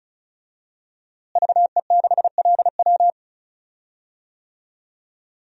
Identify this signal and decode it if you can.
Morse code, heard as a keyed tone, about 36 wpm, 700 Hz. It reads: VE6LW
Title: DXCC-100 Call Signs - 5 Characters - Repeated 3x - 35wpm